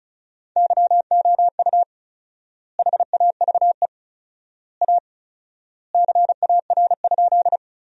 Morse code, heard as a keyed tone, about 36 wpm, 700 Hz. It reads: YOU HAVE A CAR?
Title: 4x Word-Spacing - Sentences from Top 300 Words - 35wpm